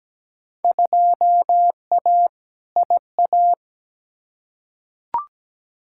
Morse code, heard as a keyed tone, about 17 wpm, 700 Hz. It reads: 2A IA
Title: Field Day Exchanges Encore - 17wpm